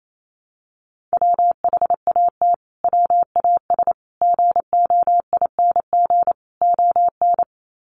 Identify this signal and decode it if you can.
Morse code, heard as a keyed tone, about 28 wpm, 700 Hz. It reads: WHAT WAS GOING ON
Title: Sets of 4 Words 28wpm